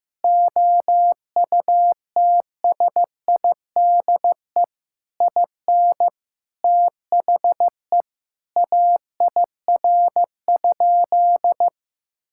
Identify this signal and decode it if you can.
Morse code, heard as a keyed tone, about 15 wpm, 700 Hz. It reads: OUTSIDE IN THE AIR?
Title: Sentences from Top 1000 Words - Review Part 6 - 15wpm